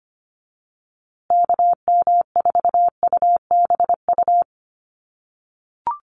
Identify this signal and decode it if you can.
Morse code, heard as a keyed tone, about 25 wpm, 700 Hz. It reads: KM4UBU
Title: USA Call Signs Repeated 3x 25wpm